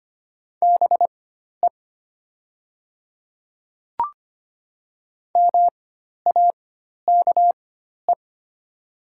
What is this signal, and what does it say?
Morse code, heard as a keyed tone, about 25 wpm, spaced as for 12 wpm, 700 Hz. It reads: BE MAKE
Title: 4x Character Spacing - Top 100 Words - 25wpm